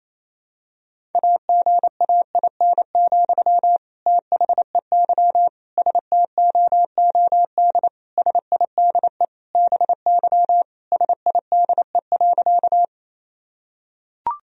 Morse code, heard as a keyed tone, about 28 wpm, 700 Hz. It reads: AGAIN, THEY STOOD SIDE BY SIDE.
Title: Sentences from Top 1000 Words - Review Part 2 - 28wpm